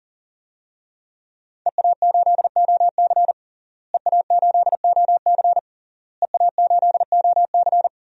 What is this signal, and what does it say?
Morse code, heard as a keyed tone, about 37 wpm, 700 Hz. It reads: EA8OC EA8OC EA8OC
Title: DXCC-100 Call Signs - 5 Characters - Repeated 3x - 40wpm